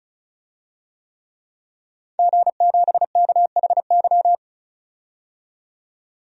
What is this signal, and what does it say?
Morse code, heard as a keyed tone, about 34 wpm, 700 Hz. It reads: G7KHY
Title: DXCC-100 Call Signs - 5 Characters - 35wpm